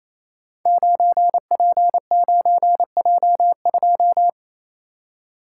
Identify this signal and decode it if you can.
Morse code, heard as a keyed tone, about 27 wpm, 700 Hz. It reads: 9P9J2
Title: Code Groups - 5 Letter:Numbers - 28wpm